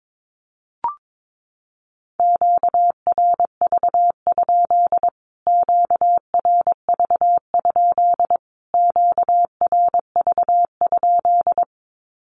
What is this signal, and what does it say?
Morse code, heard as a keyed tone, about 22 wpm, 700 Hz. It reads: QRV? QRV? QRV?
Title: Q-Codes Repeated 3x 22wpm